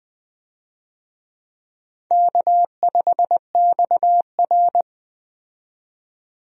Morse code, heard as a keyed tone, about 20 wpm, 700 Hz. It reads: K5XR
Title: USA Call Signs - 4 Characters - 20wpm